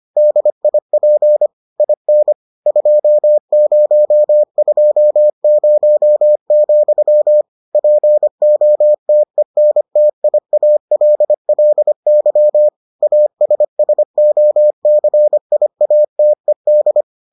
Morse code, heard as a keyed tone, about 25 wpm, 600 Hz. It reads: DIP IN 2020, POTENTIALLY ASSOCIATED